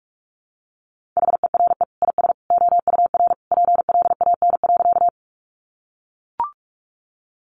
Morse code, heard as a keyed tone, about 44 wpm, 700 Hz. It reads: HERE IS OUR PLAN.
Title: Sentences from Top 500 Words 45wpm